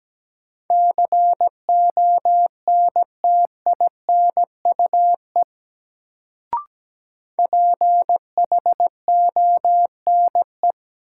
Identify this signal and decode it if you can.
Morse code, heard as a keyed tone, about 17 wpm, 700 Hz. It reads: CONTINUE PHONE
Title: Sentences from Top 1100 Words - New Word Drill Alt - 17wpm